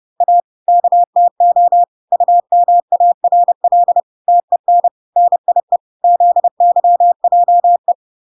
Morse code, heard as a keyed tone, about 30 wpm, 700 Hz. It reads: A KTO UMARL TEN NIE ZYJE